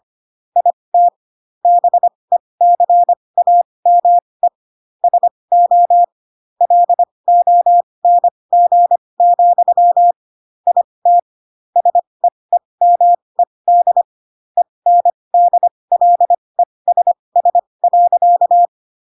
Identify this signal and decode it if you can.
Morse code, heard as a keyed tone, about 25 wpm, spaced as for 20 wpm, 700 Hz. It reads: IT BECAME SO LONG, IT SEEMED ENDLESS.
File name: Pinocchio / Pinocchio ch3 15 25 ik1hge